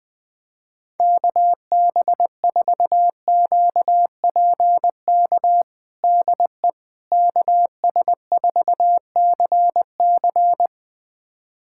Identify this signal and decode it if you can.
Morse code, heard as a keyed tone, about 20 wpm, 700 Hz. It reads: KB4QPK DE KS4CC
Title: USA Callsign DE Callsign 20wpm